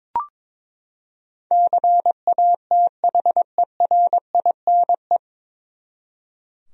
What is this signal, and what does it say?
Morse code, heard as a keyed tone, about 22 wpm, 700 Hz. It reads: CATHERINE
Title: USA Common Names - 22wpm